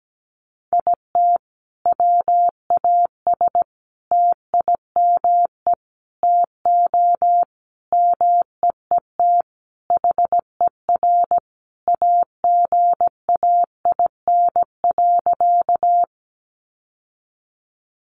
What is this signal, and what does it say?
Morse code, heard as a keyed tone, about 17 wpm, 700 Hz. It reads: IT WAS TIME TO MEET HER AGAIN.
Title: Sentences from Top 1100 Words 17wpm